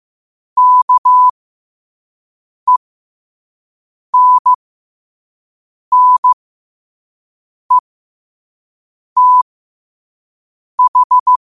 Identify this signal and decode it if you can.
Morse code, heard as a keyed tone, about 15 wpm, spaced as for 5 wpm, 1000 Hz. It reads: KENNETH
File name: qotdfiles / 5wpm